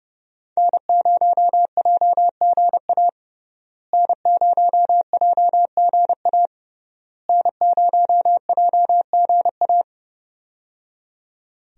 Morse code, heard as a keyed tone, about 30 wpm, 700 Hz. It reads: N0JGA N0JGA N0JGA
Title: USA Call Signs - 5 Characters - Repeated 3x - 30wpm